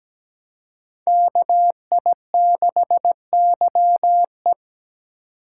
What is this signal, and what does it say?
Morse code, heard as a keyed tone, about 20 wpm, 700 Hz. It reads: KI6YE
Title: Speed Racing - USA 2x2 Call Signs 17wpm